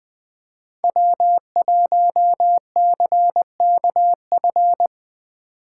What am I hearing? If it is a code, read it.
Morse code, heard as a keyed tone, about 20 wpm, 700 Hz. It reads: W1CKF
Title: Speed Racing - USA 1x3 Call Signs 17wpm